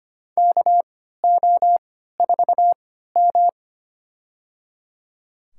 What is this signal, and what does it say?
Morse code, heard as a keyed tone, about 25 wpm, spaced as for 14 wpm, 700 Hz. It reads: KO4M
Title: USA Call Signs - 4 Characters - 3x Character Spacing - 25wpm